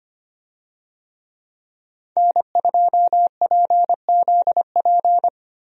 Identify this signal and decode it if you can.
Morse code, heard as a keyed tone, about 25 wpm, 700 Hz. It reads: N2PZP
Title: USA Call Signs - 5 Characters - Repeated 2x - 25wpm